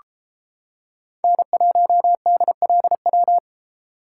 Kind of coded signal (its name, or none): Morse code